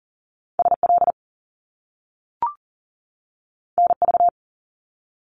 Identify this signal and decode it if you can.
Morse code, heard as a keyed tone, about 39 wpm, 700 Hz. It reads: SL NV
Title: Top Two Letter Combinations in Words 40wpm